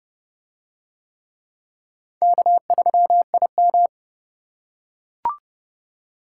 Morse code, heard as a keyed tone, about 28 wpm, 700 Hz. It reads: K3IM